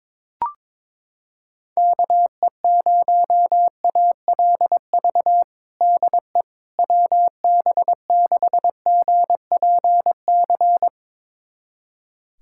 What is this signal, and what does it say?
Morse code, heard as a keyed tone, about 22 wpm, 700 Hz. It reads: KE0ALV DE WB6GPC
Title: USA Callsign DE Callsign - 6 Characters - 22wpm